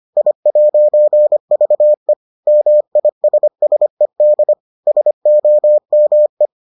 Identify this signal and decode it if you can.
Morse code, heard as a keyed tone, about 25 wpm, 600 Hz. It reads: I'VE MISSED SOME